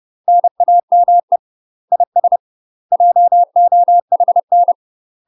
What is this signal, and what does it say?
Morse code, heard as a keyed tone, about 30 wpm, 700 Hz. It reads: NAME IS JOHN
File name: UK MASTER QSOs TRAINING / 30WPM / UK MASTER QSOs 0046